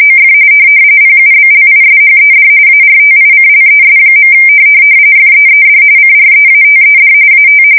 RTTY, about 45.45 baud. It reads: OF PUBLISHED ARTICLES. FEED CREATED AT
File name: fsk170 / NYT-NEWS